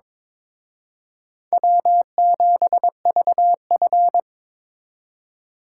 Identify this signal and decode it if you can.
Morse code, heard as a keyed tone, about 26 wpm, 700 Hz. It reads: W7VF
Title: Speed Racing - USA Call Signs 22wpm